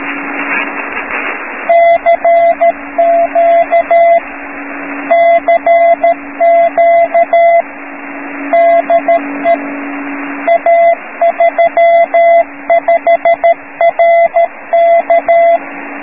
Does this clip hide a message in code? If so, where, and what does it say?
Yes: Morse code from 1.7 to 15.6 s, heard as a keyed tone, about 13 wpm, 680 Hz, with radio noise. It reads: CQ CQ DE A35RK